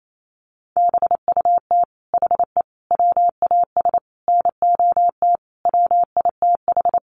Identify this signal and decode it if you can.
Morse code, heard as a keyed tone, about 28 wpm, 700 Hz. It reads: BUT HE WAS NOT WITH
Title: Sentences from Top 1500 Words 28wpm